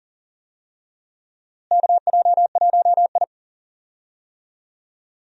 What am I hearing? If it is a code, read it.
Morse code, heard as a keyed tone, about 39 wpm, 700 Hz. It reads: KJ1I
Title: USA Call Signs - 4 Characters - Repeated 3x - 40wpm